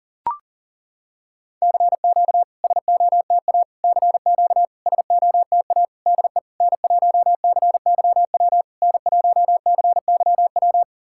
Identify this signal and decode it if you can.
Morse code, heard as a keyed tone, about 39 wpm, 690 Hz. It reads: CQ SOTA CQ SOTA DE N1CYW N1CYW
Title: SOTA QSO Practice 40wpm